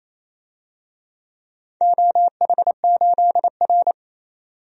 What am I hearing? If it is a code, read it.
Morse code, heard as a keyed tone, about 28 wpm, 700 Hz. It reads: OH8R
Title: DXCC-100 Call Signs - 4 Characters - 28wpm